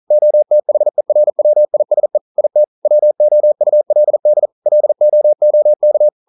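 Morse code, heard as a keyed tone, about 40 wpm, 600 Hz. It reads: OTHERWISE IT WOULD LOOK
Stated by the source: Morse code at 40 wpm, practice text, CW med Gnister.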